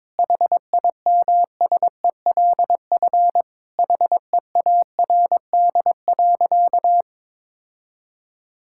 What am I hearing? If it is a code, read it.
Morse code, heard as a keyed tone, about 22 wpm, 700 Hz. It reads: HIMSELF HEARD.
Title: Sentences from Top 1600 Words 22wpm